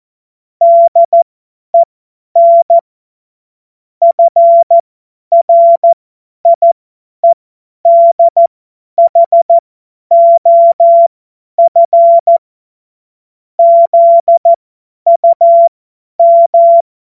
Morse code, heard as a keyed tone, about 14 wpm, spaced as for 10 wpm, 680 Hz. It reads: DEN FRIEDHOF ZUM